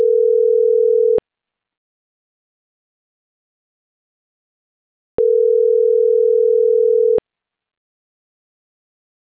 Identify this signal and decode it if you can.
Ringback tone (North America), 440 + 480 Hz.